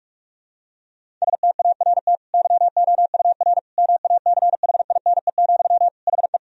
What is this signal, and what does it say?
Morse code, heard as a keyed tone, about 45 wpm, 700 Hz. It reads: START YOUR MACHINE, HE
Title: Sentences from Top 1000 Words - Review Part 2 - 45wpm